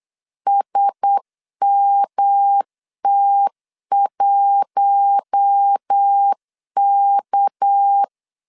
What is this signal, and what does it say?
Morse code, heard as a keyed tone, about 8 wpm, 800 Hz. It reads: SMT1K
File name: CW / DET / 1 6 1500C S B 8wpm 5142